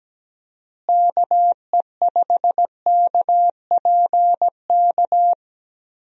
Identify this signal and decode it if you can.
Morse code, heard as a keyed tone, about 17 wpm, 700 Hz. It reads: KE5KPK
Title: USA Call Signs - 6 Characters - Repeated 2x - 17wpm